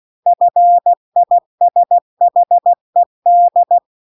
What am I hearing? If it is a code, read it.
Morse code, heard as a keyed tone, about 16 wpm, 700 Hz. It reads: FISHED
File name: CW-The-Big-Twoheaded-River-2-16WPM